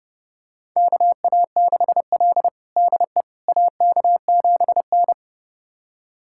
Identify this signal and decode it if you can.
Morse code, heard as a keyed tone, about 31 wpm, 700 Hz. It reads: KA6L DE AK7N
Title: USA Callsign DE Callsign - 4 Characters - 30wpm